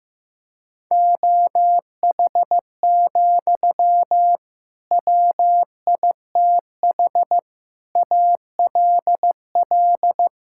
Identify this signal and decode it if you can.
Morse code, heard as a keyed tone, about 15 wpm, 700 Hz. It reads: OH, WITH ALL
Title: Sentences from Top 1600 Words 15wpm